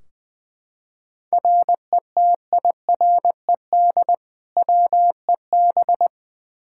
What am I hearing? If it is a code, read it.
Morse code, heard as a keyed tone, about 20 wpm, 700 Hz. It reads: RETIRED WEB
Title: Practice QSO - Exchange 3 - More Info - 20wpm